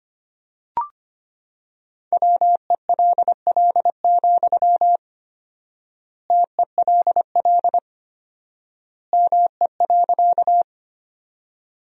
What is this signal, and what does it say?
Morse code, heard as a keyed tone, about 25 wpm, 700 Hz. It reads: WELL, TELL ME.
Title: 4x Word-Spacing - Sentences from Top 200 Words - 25wpm